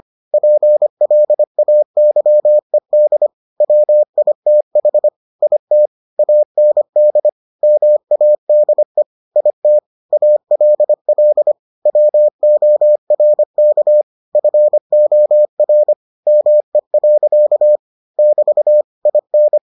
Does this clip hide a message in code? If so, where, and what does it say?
Yes: Morse code from 0.3 to 19.6 s, heard as a keyed tone, about 25 wpm, 600 Hz. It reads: PLAYED WITH IT AND MADE IT ALL WORK FOR ME. <BT> IN